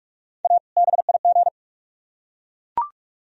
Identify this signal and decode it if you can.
Morse code, heard as a keyed tone, about 44 wpm, 700 Hz. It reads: A BIG